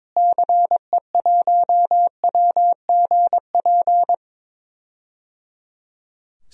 Morse code, heard as a keyed tone, about 22 wpm, 700 Hz. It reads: CE1WGP
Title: DXCC-100 Call Signs - 6 Characters - Repeated 3x - 22wpm